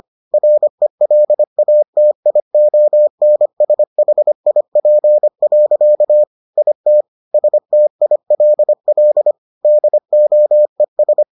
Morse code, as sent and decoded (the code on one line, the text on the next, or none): .-. .-.. .- - .. --- -. ... .... .. .--. .-.-.- / .. - / ... - .. .-.. .-.. / -.. --- . ...
RELATIONSHIP. IT STILL DOES